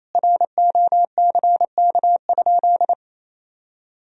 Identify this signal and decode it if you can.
Morse code, heard as a keyed tone, about 28 wpm, 700 Hz. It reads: ROCK?
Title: Sentences from Top 1000 Words - Review Part 2 - 28wpm